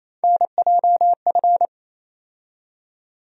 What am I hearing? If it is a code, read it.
Morse code, heard as a keyed tone, about 28 wpm, 700 Hz. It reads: NJF
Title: USA Call Sign Postfixes - 3 Letters 28wpm